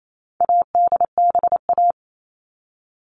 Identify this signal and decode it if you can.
Morse code, heard as a keyed tone, about 28 wpm, 700 Hz. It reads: ADBA
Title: All Cut Numbers - 4-Digit 28wpm